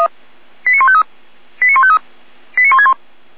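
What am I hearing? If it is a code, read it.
Five-tone selective call (ZVEI), heard as quick runs of five tones, tones of about 70 ms. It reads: 89153 89153 89161
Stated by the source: ZVEI five-tone selective call received by SDR.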